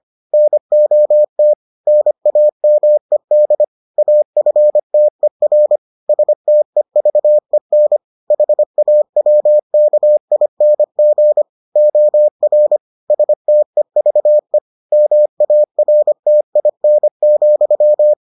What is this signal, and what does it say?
Morse code, heard as a keyed tone, about 25 wpm, 600 Hz. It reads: NOT NAMED AFTER STEVEN HAWKING OR STEVE MARTIN,